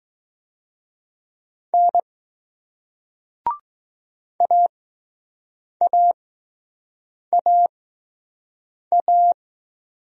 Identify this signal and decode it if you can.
Morse code, heard as a keyed tone, about 20 wpm, spaced as for 6 wpm, 700 Hz. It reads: N AAAA